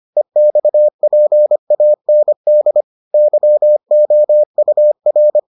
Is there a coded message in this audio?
Yes: Morse code from 0.2 to 5.4 s, heard as a keyed tone, about 25 wpm, 600 Hz. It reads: EXPAND YOUR